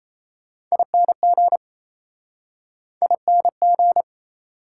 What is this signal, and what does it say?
Morse code, heard as a keyed tone, about 37 wpm, 700 Hz. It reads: ING ING